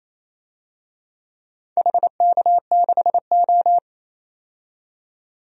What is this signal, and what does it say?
Morse code, heard as a keyed tone, about 28 wpm, 700 Hz. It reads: HK6O
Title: DXCC-100 Call Signs - 4 Characters - 28wpm